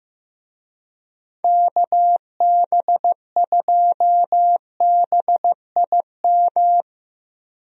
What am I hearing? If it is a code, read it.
Morse code, heard as a keyed tone, about 15 wpm, 700 Hz. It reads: KB2BIM
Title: USA Call Signs - 6 Characters - Repeated 2x - 15wpm